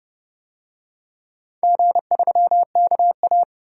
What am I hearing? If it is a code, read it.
Morse code, heard as a keyed tone, about 31 wpm, 700 Hz. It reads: G3KA
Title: DXCC-100 Call Signs - 4 Characters - 30wpm